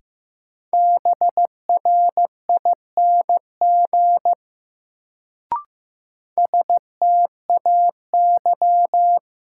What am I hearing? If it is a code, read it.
Morse code, heard as a keyed tone, about 15 wpm, 700 Hz. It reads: BRING STAY
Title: Sentences from Top 500 Words - New Word Drill - 15wpm